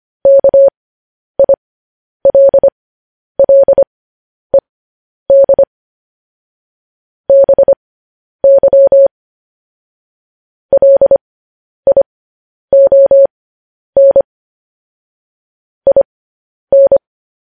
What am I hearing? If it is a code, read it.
Morse code, heard as a keyed tone, about 25 wpm, spaced as for 10 wpm, 550 Hz. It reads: KILLED BY LION IN
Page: News Headlines in Morse Code at 10 WPM